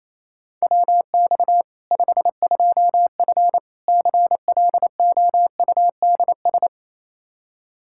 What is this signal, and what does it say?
Morse code, heard as a keyed tone, about 28 wpm, 700 Hz. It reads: WX 52F CLOUDS
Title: K1USN SST QSO Practice 28wpm